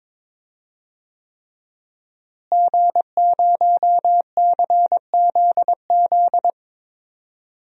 Morse code, heard as a keyed tone, about 22 wpm, 700 Hz. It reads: G0CZZ
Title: DXCC-100 Call Signs - 5 Characters - Repeated 3x - 22wpm